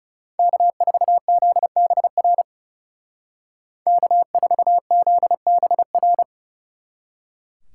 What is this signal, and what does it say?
Morse code, heard as a keyed tone, about 39 wpm, 700 Hz. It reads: K4ZBR K4ZBR